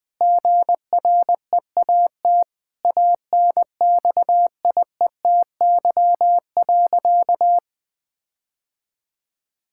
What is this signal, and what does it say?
Morse code, heard as a keyed tone, about 20 wpm, 700 Hz. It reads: GREAT ANXIETY.
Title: Sentences from Top 1600 Words 20wpm